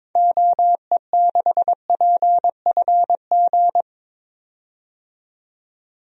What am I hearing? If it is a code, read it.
Morse code, heard as a keyed tone, about 22 wpm, 700 Hz. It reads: OE6PFG
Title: DXCC-100 Call Signs - 6 Characters - 22wpm